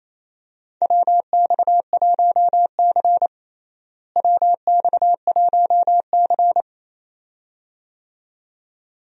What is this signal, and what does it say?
Morse code, heard as a keyed tone, about 28 wpm, 700 Hz. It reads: WX1C WX1C
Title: USA Call Signs - 4 Characters - Repeated 2x - 28wpm